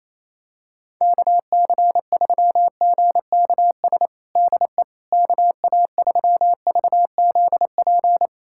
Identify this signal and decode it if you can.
Morse code, heard as a keyed tone, about 28 wpm, 700 Hz. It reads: KC3GKS DE KA3VZP
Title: USA Callsign DE Callsign - 6 Characters - 28wpm